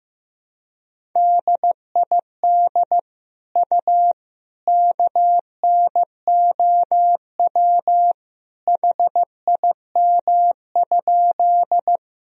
Morse code, heard as a keyed tone, about 16 wpm, 700 Hz. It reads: DID U KNOW HIM?
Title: Sentences from Top 100 Words and Abbreviations 15wpm